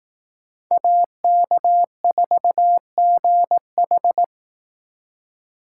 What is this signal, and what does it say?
Morse code, heard as a keyed tone, about 19 wpm, 700 Hz. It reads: AK4GH